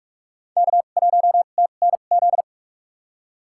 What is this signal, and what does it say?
Morse code, heard as a keyed tone, about 46 wpm, spaced as for 33 wpm, 700 Hz. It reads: K1TNZ
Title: USA Call Signs - 5 Characters - 2x Character Spacing - 45wpm